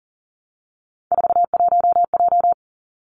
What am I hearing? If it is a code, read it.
Morse code, heard as a keyed tone, about 37 wpm, 700 Hz. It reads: 41J
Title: Three Challenging Letter-Number Combinations 40wpm